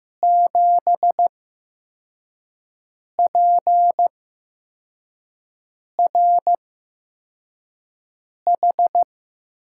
Morse code, heard as a keyed tone, about 15 wpm, spaced as for 4 wpm, 700 Hz. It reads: 7PRH
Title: USA Call Signs - 6 Characters - 8x Character Spacing - 15wpm